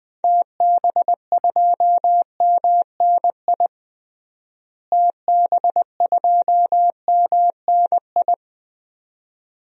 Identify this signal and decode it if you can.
Morse code, heard as a keyed tone, about 20 wpm, 700 Hz. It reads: TB2MNI TB2MNI
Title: DXCC-100 Call Signs - 6 Characters - Repeated 2x - 20wpm